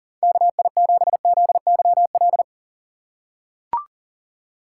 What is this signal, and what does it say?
Morse code, heard as a keyed tone, about 40 wpm, 690 Hz. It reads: KI7ZYL
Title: USA Call Signs - 4x Character Spacing - 40wpm